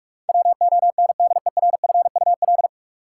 Morse code, heard as a keyed tone, about 46 wpm, 700 Hz. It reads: WONDERFUL
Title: Sentences from Top 1100 Words - New Word Drill Alt - 45wpm